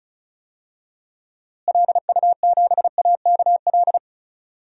Morse code, heard as a keyed tone, about 34 wpm, 700 Hz. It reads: LU7AKL
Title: DXCC-100 Call Signs - 6 Characters - Repeated 3x - 35wpm